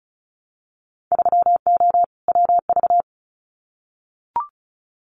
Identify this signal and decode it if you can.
Morse code, heard as a keyed tone, about 36 wpm, 690 Hz. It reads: 3O WV
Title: Winter Field Day Exchanges 35wpm